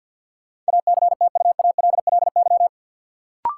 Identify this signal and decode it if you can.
Morse code, heard as a keyed tone, about 53 wpm, 700 Hz. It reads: ACTUALLY
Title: Sentences from Top 700 Words - New Word Drill - 50wpm